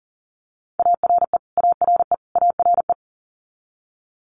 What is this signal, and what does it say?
Morse code, heard as a keyed tone, about 40 wpm, 700 Hz. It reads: ARE ARE ARE
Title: Top 200 Words Repeated 40wpm